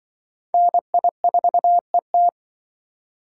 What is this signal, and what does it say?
Morse code, heard as a keyed tone, about 24 wpm, 700 Hz. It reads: NI4ET
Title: Speed Racing - USA 2x2 Call Signs 20wpm